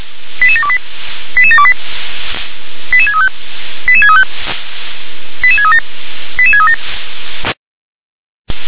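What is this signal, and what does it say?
Five-tone selective call (ZVEI), heard as quick runs of five tones, tones of about 70 ms. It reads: 88728 88728 88635 88635 88637 88637
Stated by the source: Five-tone ZVEI selective call received on SDR.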